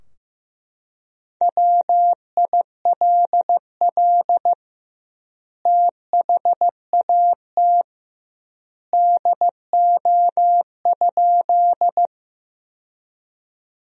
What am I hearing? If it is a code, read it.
Morse code, heard as a keyed tone, about 16 wpm, 700 Hz. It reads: WILL THAT DO?
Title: 2x Word-Spacing - Sentences from Top 100 Words - 15wpm